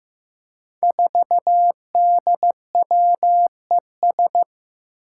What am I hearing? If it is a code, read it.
Morse code, heard as a keyed tone, about 15 wpm, 700 Hz. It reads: 4DWES
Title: Five Challenging Letter-Number Combinations 15wpm